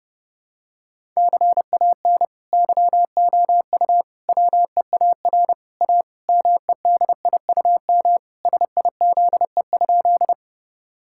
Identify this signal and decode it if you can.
Morse code, heard as a keyed tone, about 32 wpm, 700 Hz. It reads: CAN YOU WEAR A MEDIUM SIZE?
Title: Sentences from Top 1000 Words - Review Part 6 - 30wpm